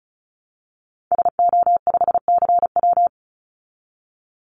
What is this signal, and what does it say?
Morse code, heard as a keyed tone, about 34 wpm, 700 Hz. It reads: SO5CW
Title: Call Signs Repeated 3x - 35wpm